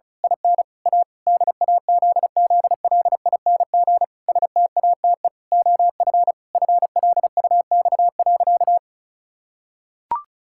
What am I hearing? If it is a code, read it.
Morse code, heard as a keyed tone, about 34 wpm, 700 Hz. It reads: IN A DAZZLING STATE OF FLUX.
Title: Pangram Challenge v2 - 35wpm